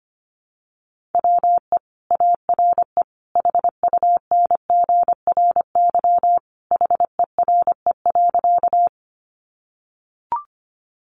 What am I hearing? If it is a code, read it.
Morse code, heard as a keyed tone, about 25 wpm, 700 Hz. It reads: WE ARE HUNGRY HERE.
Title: Sentences from Top 1400 Words 25wpm